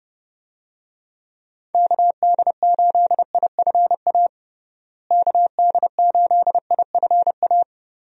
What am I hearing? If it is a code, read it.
Morse code, heard as a keyed tone, about 30 wpm, 700 Hz. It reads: KD8IFA KD8IFA
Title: USA Call Signs - 6 Characters - Repeated 3x - 30wpm